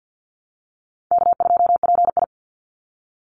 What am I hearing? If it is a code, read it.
Morse code, heard as a keyed tone, about 50 wpm, 700 Hz. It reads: K2PI
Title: Call Signs - 50wpm